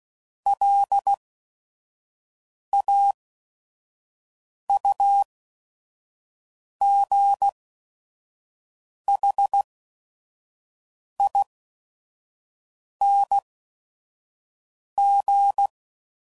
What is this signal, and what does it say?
Morse code, heard as a keyed tone, about 16 wpm, spaced as for 5 wpm, 800 Hz. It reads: LAUGHING
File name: Suffixes / Special ING's / VerySlowToFaster ING's